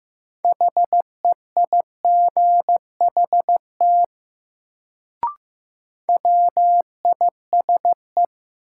Morse code, heard as a keyed tone, about 15 wpm, 700 Hz. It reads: HEIGHT WISE
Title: Sentences from Top 1600 Words - New Word Drill Alt - 15wpm